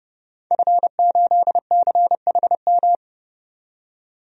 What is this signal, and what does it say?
Morse code, heard as a keyed tone, about 32 wpm, 700 Hz. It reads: F8CHM
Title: DXCC-100 Call Signs - 5 Characters - 30wpm